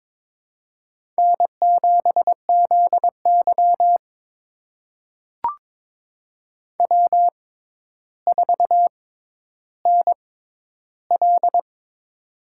Morse code, heard as a keyed tone, about 22 wpm, spaced as for 11 wpm, 700 Hz. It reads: N7ZY W4NL
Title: USA Call Signs - 4 Characters - 6x Character Spacing - 22wpm